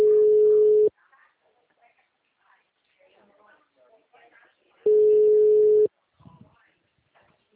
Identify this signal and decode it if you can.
Ringback tone (Europe, 425 Hz), with speech.